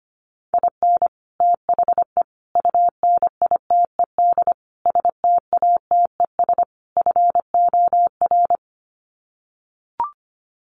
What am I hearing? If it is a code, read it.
Morse code, heard as a keyed tone, about 25 wpm, 700 Hz. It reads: IN THE UNITED STATES FOR
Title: Sets of 5 Words Encore2 25wpm